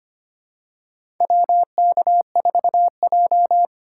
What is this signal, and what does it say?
Morse code, heard as a keyed tone, about 25 wpm, 700 Hz. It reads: WK4J
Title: USA Call Signs - 4 Characters - 3x Character Spacing - 25wpm